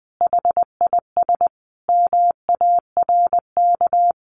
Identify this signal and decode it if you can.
Morse code, heard as a keyed tone, about 20 wpm, 700 Hz. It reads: HIS MARK
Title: Sentences from Top 1100 Words 20wpm